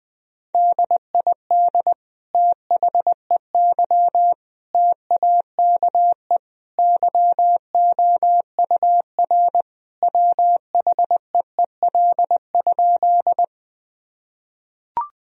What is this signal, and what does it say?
Morse code, heard as a keyed tone, about 20 wpm, 700 Hz. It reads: DID THEY TAKE YOUR WHEEL?
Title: Sentences from Top 1000 Words - Review Part 3 - 20wpm